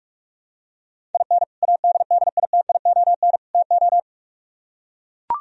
Morse code, heard as a keyed tone, about 44 wpm, 700 Hz. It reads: IN ADDITION TO